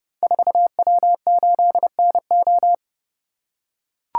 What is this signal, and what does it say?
Morse code, heard as a keyed tone, about 29 wpm, 690 Hz. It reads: 4W8NO
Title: Code Groups - 5 Letter:Numbers - 30wpm